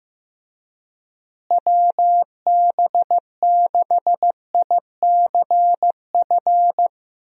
Morse code, heard as a keyed tone, about 14 wpm, 700 Hz. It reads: WB6ICF
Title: USA Call Signs - 6 Characters - Repeated 2x - 15wpm